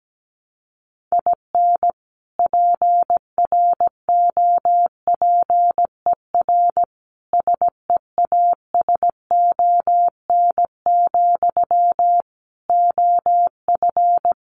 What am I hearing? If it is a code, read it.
Morse code, heard as a keyed tone, about 17 wpm, 700 Hz. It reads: IN PROPER SEASON, OF
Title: Sentences from Top 1400 Words 17wpm